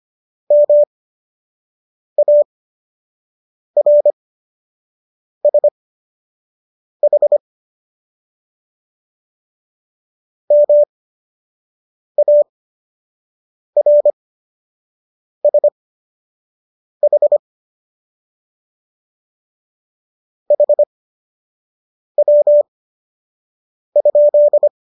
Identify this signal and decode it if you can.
Morse code, heard as a keyed tone, about 25 wpm, spaced as for 6 wpm, 590 Hz. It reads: MARSH MARSH HW?